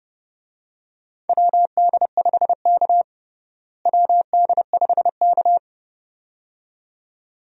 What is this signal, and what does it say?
Morse code, heard as a keyed tone, about 31 wpm, 700 Hz. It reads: WD5K WD5K
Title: USA Call Signs - 4 Characters - Repeated 2x - 30wpm